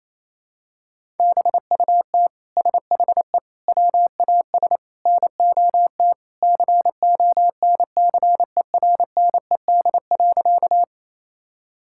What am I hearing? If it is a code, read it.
Morse code, heard as a keyed tone, about 28 wpm, 700 Hz. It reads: BUT SHE WAS NOT CONCERNED.
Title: Sentences from Top 1000 Words - Review Part 6 - 28wpm